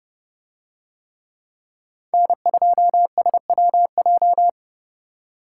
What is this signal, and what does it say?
Morse code, heard as a keyed tone, about 29 wpm, 700 Hz. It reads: N2SWJ